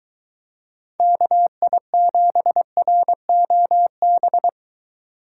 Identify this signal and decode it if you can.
Morse code, heard as a keyed tone, about 23 wpm, 700 Hz. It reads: KI7ROB